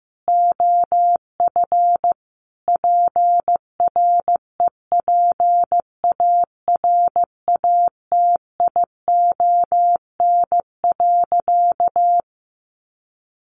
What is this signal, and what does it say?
Morse code, heard as a keyed tone, about 15 wpm, 700 Hz. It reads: OF PREPARATION.
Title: Sentences from Top 1400 Words 15wpm